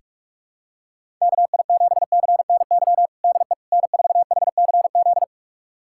Morse code, heard as a keyed tone, about 44 wpm, 700 Hz. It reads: KI7CNY DE N4HCZ
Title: USA Callsign DE Callsign 45wpm